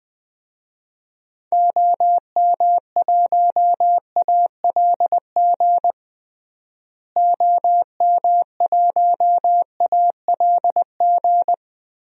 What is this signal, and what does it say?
Morse code, heard as a keyed tone, about 20 wpm, 700 Hz. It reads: OM1ALG OM1ALG
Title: DXCC-100 Call Signs - 6 Characters - Repeated 2x - 20wpm